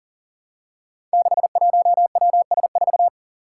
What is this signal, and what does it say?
Morse code, heard as a keyed tone, about 43 wpm, 700 Hz. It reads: 61WS4
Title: Five Challenging Letter-Number Combinations 40wpm